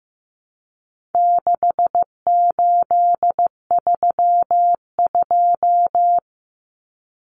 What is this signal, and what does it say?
Morse code, heard as a keyed tone, about 15 wpm, 700 Hz. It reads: 6832
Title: Numbers - 4-Digit 15wpm